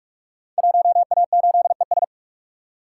Morse code, heard as a keyed tone, about 44 wpm, 700 Hz. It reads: 1A8ES
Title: Code Groups - 5 Letter:Numbers - 45wpm